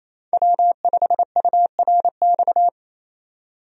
Morse code, heard as a keyed tone, about 28 wpm, 700 Hz. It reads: W5URX
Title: USA Call Signs - 5 Characters - 28wpm